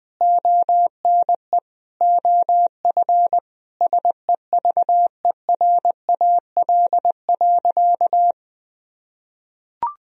Morse code, heard as a keyed tone, about 20 wpm, 700 Hz. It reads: ONE OF SEVERAL.
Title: Sentences from Top 1000 Words - Review Part 2 - 20wpm